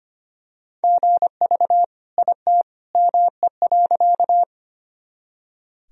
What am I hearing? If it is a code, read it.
Morse code, heard as a keyed tone, about 25 wpm, 700 Hz. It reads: GV IT ME.
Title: Sentences from Top 200 Words and Abbreviations 25wpm